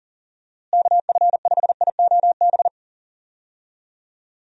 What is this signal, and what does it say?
Morse code, heard as a keyed tone, about 40 wpm, 700 Hz. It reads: KF5IOB
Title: USA Call Signs - 6 Characters - Repeated 2x - 40wpm